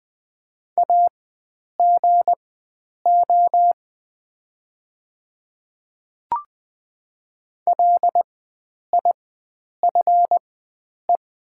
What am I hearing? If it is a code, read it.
Morse code, heard as a keyed tone, about 20 wpm, spaced as for 9 wpm, 690 Hz. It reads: AGO LIFE
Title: 4x Character Spacing - Top 500 Words - 20wpm